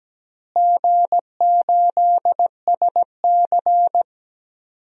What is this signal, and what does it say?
Morse code, heard as a keyed tone, about 17 wpm, 700 Hz. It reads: G8SC
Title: DXCC-100 Call Signs - 4 Characters - Repeated 3x - 17wpm